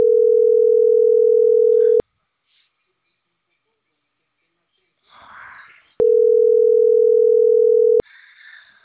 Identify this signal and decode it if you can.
Ringback tone (North America), 440 + 480 Hz, with speech.